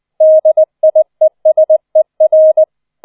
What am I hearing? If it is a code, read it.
Morse code, heard as a keyed tone, about 20 wpm, 630 Hz. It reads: DIESER